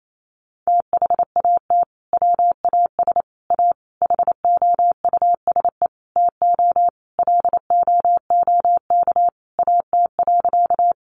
Morse code, heard as a keyed tone, about 28 wpm, 700 Hz. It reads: THAT WAS A HOUSE TO LOOK AT.
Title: Sentences from Top 200 Words 28wpm